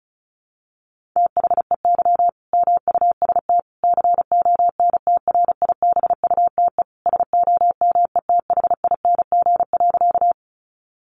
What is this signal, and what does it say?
Morse code, heard as a keyed tone, about 34 wpm, 700 Hz. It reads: THEY MUST CONTRIBUTE SOMETHING.
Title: Sentences from Top 1400 Words 35wpm